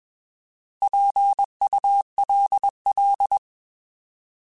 Morse code, heard as a keyed tone, about 21 wpm, 800 Hz. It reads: PULL
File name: Alphabetical / Alphabetical mp3s / Ps @20